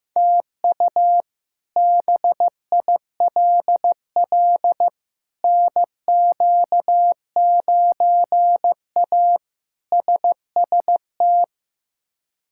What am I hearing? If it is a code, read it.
Morse code, heard as a keyed tone, about 16 wpm, 700 Hz. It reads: TU BILL NQ9A SST
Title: K1USN SST QSO Practice - 15wpm